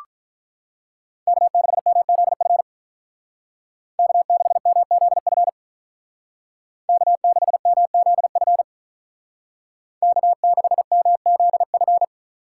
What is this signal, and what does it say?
Morse code, heard as a keyed tone, about 46 wpm, 700 Hz. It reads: K6MZF K6MZF K6MZF K6MZF